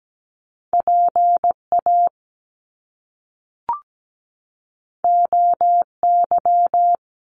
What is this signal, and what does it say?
Morse code, heard as a keyed tone, about 17 wpm, 700 Hz. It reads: PA OY
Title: Top Two Letter Combinations in Words 17wpm